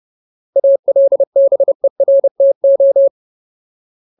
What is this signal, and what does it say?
Morse code, heard as a keyed tone, about 30 wpm, 550 Hz. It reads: ALBERTO
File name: Ultimate Word List / 30WPM Ultimate